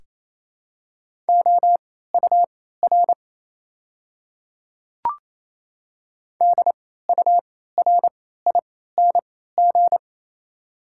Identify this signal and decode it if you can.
Morse code, heard as a keyed tone, about 27 wpm, spaced as for 16 wpm, 690 Hz. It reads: OUR DURING
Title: 3x Character Spacing - Top 500 Words - 28wpm